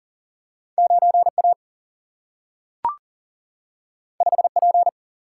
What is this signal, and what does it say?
Morse code, heard as a keyed tone, about 40 wpm, 700 Hz. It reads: 9 A 5 P
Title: Single-Letter Single-Number 40wpm